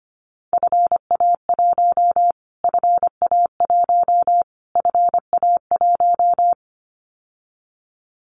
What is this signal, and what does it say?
Morse code, heard as a keyed tone, about 25 wpm, 700 Hz. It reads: FA1 FA1 FA1
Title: Callsign Prefixes for Top 25 DXCC - Repeated 3x - 25wpm